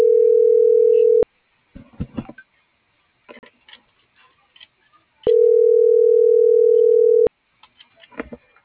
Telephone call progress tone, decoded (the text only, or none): ringback tone